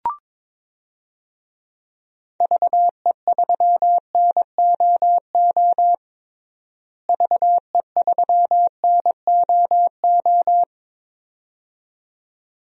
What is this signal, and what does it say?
Morse code, heard as a keyed tone, about 22 wpm, 700 Hz. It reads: VE3NOO VE3NOO
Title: DXCC-100 Call Signs - 6 Characters - Repeated 2x - 22wpm